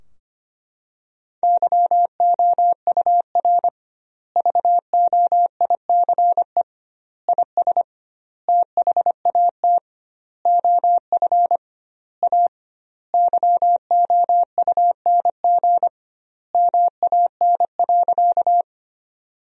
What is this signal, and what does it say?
Morse code, heard as a keyed tone, about 25 wpm, 700 Hz. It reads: YOUR VOICE IS THAT OF A YOUNG MAN.
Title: 2x Word-Spacing - Sentences from Top 500 Words - 25wpm